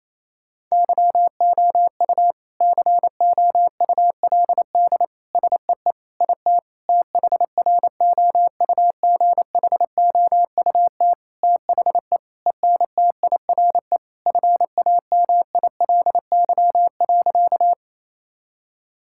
Morse code, heard as a keyed tone, about 27 wpm, 700 Hz. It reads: YOU COULD SEE IT THROUGHOUT THE ENTIRE FAMILY.
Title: Sentences from Top 1000 Words - Review Part 5 - 28wpm